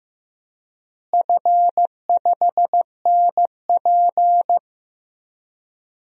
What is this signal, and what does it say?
Morse code, heard as a keyed tone, about 15 wpm, 700 Hz. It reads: F5NP
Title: DXCC-100 Call Signs - 4 Characters - Repeated 3x - 15wpm